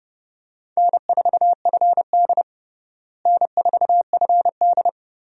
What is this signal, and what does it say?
Morse code, heard as a keyed tone, about 32 wpm, 700 Hz. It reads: N4FD N4FD
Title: USA Call Signs - 4 Characters - Repeated 2x - 30wpm